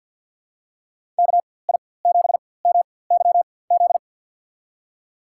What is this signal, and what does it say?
Morse code, heard as a keyed tone, about 47 wpm, spaced as for 23 wpm, 700 Hz. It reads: KI7MYZ
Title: USA Call Signs - 6 Characters - 4x Character Spacing - 50wpm